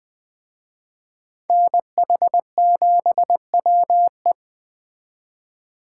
Morse code, heard as a keyed tone, about 21 wpm, 700 Hz. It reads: NH7WE
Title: Speed Racing - USA 2x2 Call Signs 17wpm